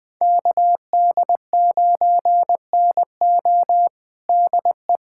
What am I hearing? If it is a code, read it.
Morse code, heard as a keyed tone, about 20 wpm, 700 Hz. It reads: KD9NO DE
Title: USA Callsign DE Callsign - 5 Characters - 20wpm